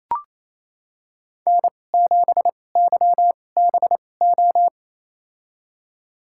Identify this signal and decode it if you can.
Morse code, heard as a keyed tone, about 28 wpm, spaced as for 20 wpm, 700 Hz. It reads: N7YBO
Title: USA Call Signs - 5 Characters - 2x Character Spacing - 28wpm